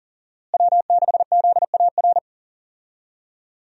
Morse code, heard as a keyed tone, about 40 wpm, 700 Hz. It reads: W6ZAR
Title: USA Call Signs - 5 Characters - Repeated 2x - 40wpm